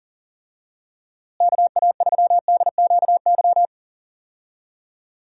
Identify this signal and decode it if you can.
Morse code, heard as a keyed tone, about 40 wpm, 700 Hz. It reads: KA3DQY
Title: USA Call Signs - 6 Characters - Repeated 3x - 40wpm